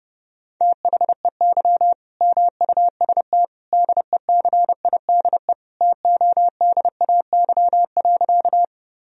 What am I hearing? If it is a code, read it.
Morse code, heard as a keyed tone, about 30 wpm, 700 Hz. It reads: THEY MUST DECIDE TODAY.
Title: Sentences from Top 1000 Words - Review Part 3 - 30wpm